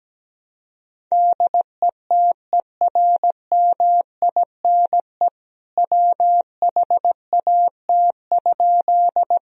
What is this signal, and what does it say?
Morse code, heard as a keyed tone, about 17 wpm, 700 Hz. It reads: DETERMINE WHAT?
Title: Sentences from Top 1000 Words - Review Part 4 - 17wpm